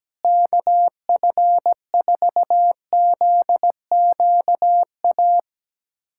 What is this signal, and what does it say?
Morse code, heard as a keyed tone, about 17 wpm, 700 Hz. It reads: KF4ZQA
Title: USA Callsign DE Callsign 17wpm